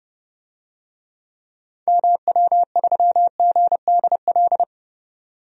Morse code, heard as a keyed tone, about 29 wpm, 700 Hz. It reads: MW3GDL
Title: DXCC-100 Call Signs - 6 Characters - 30wpm